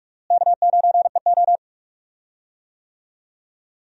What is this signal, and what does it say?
Morse code, heard as a keyed tone, about 41 wpm, 700 Hz. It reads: K9EO
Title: USA Call Signs - 4 Characters - Repeated 3x - 45wpm